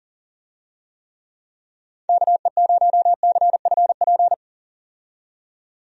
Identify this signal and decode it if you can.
Morse code, heard as a keyed tone, about 40 wpm, 700 Hz. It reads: KE0CFP
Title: USA Call Signs - 6 Characters - 40wpm